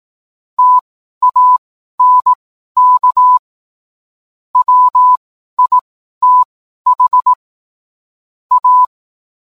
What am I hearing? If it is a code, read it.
Morse code, heard as a keyed tone, about 18 wpm, spaced as for 12 wpm, 1000 Hz. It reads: TANK WITH A